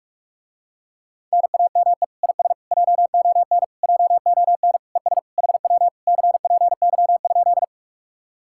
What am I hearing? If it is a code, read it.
Morse code, heard as a keyed tone, about 49 wpm, 700 Hz. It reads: NAME IS JON JON ES HW CPY?